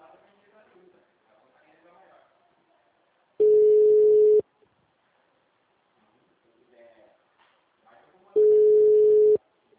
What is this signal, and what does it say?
Ringback tone (Europe, 425 Hz), with speech.
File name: out- -2003- - - .3751